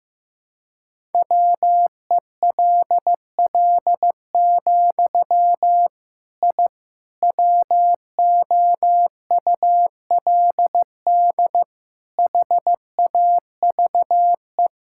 Morse code, heard as a keyed tone, about 15 wpm, 700 Hz. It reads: WELL, I WOULD HAVE
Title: Sentences from Top 1000 Words - Review Part 5 - 15wpm